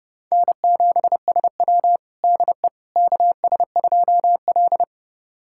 Morse code, heard as a keyed tone, about 30 wpm, 700 Hz. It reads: N7SW DE KS2L
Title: USA Callsign DE Callsign - 4 Characters - 30wpm